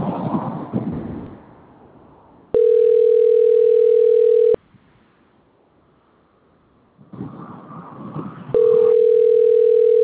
Ringback tone (North America), 440 + 480 Hz.